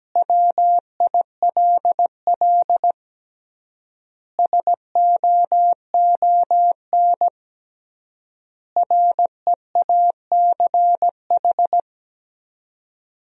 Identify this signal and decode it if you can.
Morse code, heard as a keyed tone, about 17 wpm, 700 Hz. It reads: WILL SOON REACH
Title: 3x Word-Spacing - Sentences from Top 400 Words - 17wpm